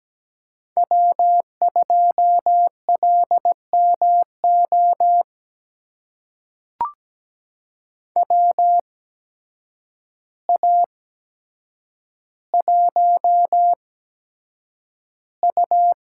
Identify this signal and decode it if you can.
Morse code, heard as a keyed tone, about 17 wpm, spaced as for 8 wpm, 700 Hz. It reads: W2LMO WA1U
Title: USA Call Signs - 8x Character Spacing - 17wpm